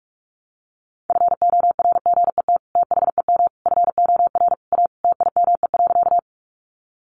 Morse code, heard as a keyed tone, about 44 wpm, 700 Hz. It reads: FORGET THEM FOR A TIME.
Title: Sentences from Top 1200 Words 45wpm